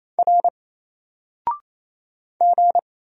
Morse code, heard as a keyed tone, about 27 wpm, 700 Hz. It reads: R G
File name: Single-Letter-Number-28wpm